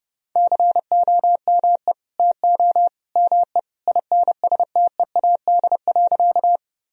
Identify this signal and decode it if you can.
Morse code, heard as a keyed tone, about 32 wpm, 700 Hz. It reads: COME TO ME INSTEAD.
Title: Sentences from Top 1000 Words - Review Part 4 - 30wpm